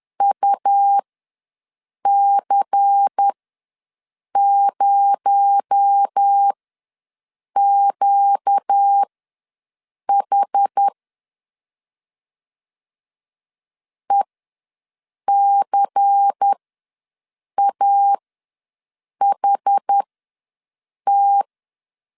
Morse code, heard as a keyed tone, about 10 wpm, spaced as for 6 wpm, 800 Hz. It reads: UC0QH ECAHT
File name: LEZIONI / 6 PKX0 / 6 PKX0 ws 10wpm 4220